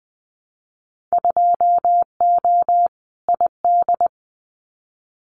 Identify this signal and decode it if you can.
Morse code, heard as a keyed tone, about 20 wpm, 700 Hz. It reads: 2O ID
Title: Winter Field Day Exchanges 20wpm